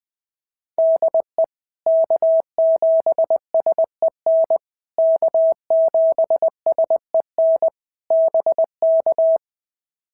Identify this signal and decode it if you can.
Morse code, heard as a keyed tone, about 20 wpm, 650 Hz. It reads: DE K7SEN K7SEN BK
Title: POTA QSO Practice - 20wpm